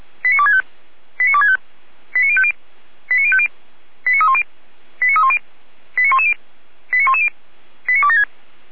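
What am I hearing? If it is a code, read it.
Five-tone selective call (ZVEI), heard as quick runs of five tones, tones of about 70 ms. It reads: 89265 89265 89050 89050 89310 89310 89110 89110 89276
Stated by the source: ZVEI five-tone selective call received by SDR, from the feuerwehr-orlamuende.org (fire brigade Orlamünde) audio archive.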